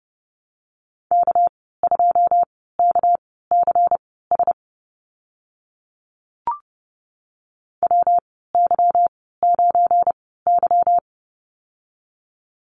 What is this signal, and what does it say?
Morse code, heard as a keyed tone, about 30 wpm, spaced as for 17 wpm, 700 Hz. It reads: K2KCS WY9Y
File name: Lesson 37g - Letter x - Call Signs 30wpm